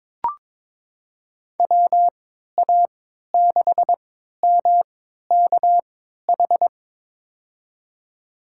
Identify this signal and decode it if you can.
Morse code, heard as a keyed tone, about 22 wpm, spaced as for 13 wpm, 700 Hz. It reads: WA6MKH
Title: USA Call Signs - 6 Characters - 3x Character Spacing - 22wpm